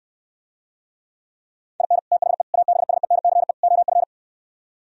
Morse code, heard as a keyed tone, about 68 wpm, 700 Hz. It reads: IN THE ABSENCE OF